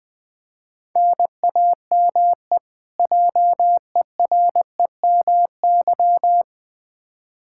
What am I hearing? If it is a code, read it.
Morse code, heard as a keyed tone, about 20 wpm, 700 Hz. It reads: NAME JEREMY
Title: K1USN SST QSO Practice 20wpm